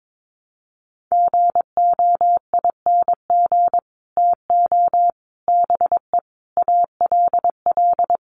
Morse code, heard as a keyed tone, about 22 wpm, 700 Hz. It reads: GOING TO BE ALL
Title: Sets of 5 Words Encore2 22wpm